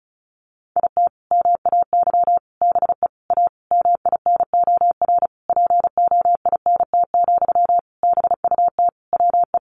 Morse code, heard as a keyed tone, about 34 wpm, 700 Hz. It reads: IT MAY BE A MINOR POINT, BUT WE
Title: Sentences from Top 1500 Words 35wpm